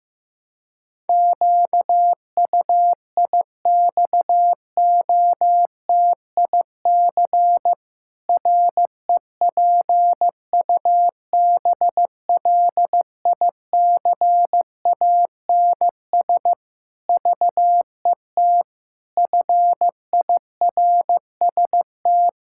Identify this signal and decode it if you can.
Morse code, heard as a keyed tone, about 16 wpm, 700 Hz. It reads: QUIXOTIC REPUBLICANS VET FIRST
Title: Pangram Challenge v2 - 15wpm